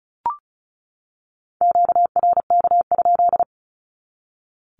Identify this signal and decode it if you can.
Morse code, heard as a keyed tone, about 34 wpm, 700 Hz. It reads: QRK?
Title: Q-Codes 35wpm